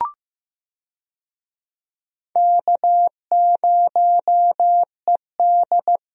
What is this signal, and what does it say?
Morse code, heard as a keyed tone, about 16 wpm, 700 Hz. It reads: K0ED
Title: USA Call Signs - 4 Characters - Repeated 3x - 15wpm